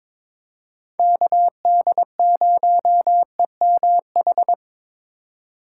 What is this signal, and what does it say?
Morse code, heard as a keyed tone, about 21 wpm, 700 Hz. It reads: KD0EMH
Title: USA Call Signs - 4x Character Spacing - 22wpm